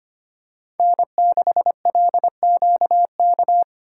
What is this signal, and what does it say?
Morse code, heard as a keyed tone, about 25 wpm, 700 Hz. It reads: N6LQK
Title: USA Call Signs - 5 Characters - 4x Character Spacing - 25wpm